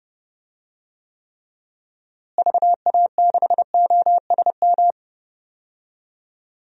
Morse code, heard as a keyed tone, about 32 wpm, 700 Hz. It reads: VA6OSM
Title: DXCC-100 Call Signs - 6 Characters - Repeated 2x - 30wpm